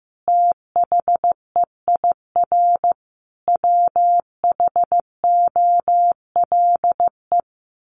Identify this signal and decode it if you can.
Morse code, heard as a keyed tone, about 14 wpm, 700 Hz. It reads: THEIR WHOLE
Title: Sentences from Top 1500 Words 15wpm